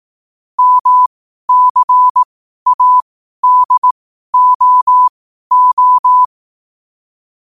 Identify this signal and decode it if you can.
Morse code, heard as a keyed tone, about 18 wpm, spaced as for 12 wpm, 1000 Hz. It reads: MCADOO
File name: qotdfiles / 12wpm